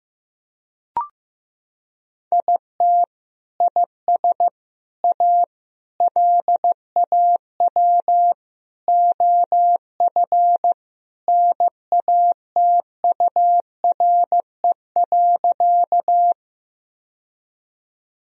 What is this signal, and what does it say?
Morse code, heard as a keyed tone, about 15 wpm, 700 Hz. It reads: IT IS A LAW OF NATURE.
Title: Sentences from Top 1000 Words - Review Part 3 - 15wpm